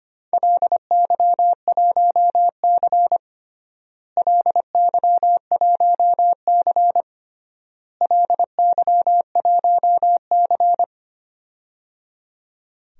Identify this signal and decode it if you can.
Morse code, heard as a keyed tone, about 25 wpm, 700 Hz. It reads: LY1C LY1C LY1C
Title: DXCC-100 Call Signs - 4 Characters - Repeated 3x - 25wpm